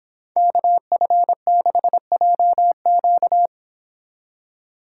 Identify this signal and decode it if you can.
Morse code, heard as a keyed tone, about 25 wpm, 700 Hz. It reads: KF6JQ